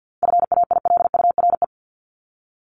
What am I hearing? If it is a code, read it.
Morse code, heard as a keyed tone, about 50 wpm, 700 Hz. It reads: FAILURE
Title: Top 751-1000 Words 50wpm